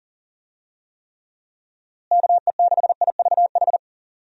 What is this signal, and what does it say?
Morse code, heard as a keyed tone, about 40 wpm, 700 Hz. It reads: KE6IVH
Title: USA Call Signs - 6 Characters - Repeated 2x - 40wpm